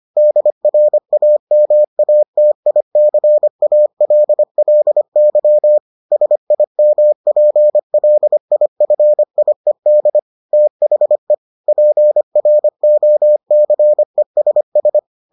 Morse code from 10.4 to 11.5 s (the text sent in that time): THE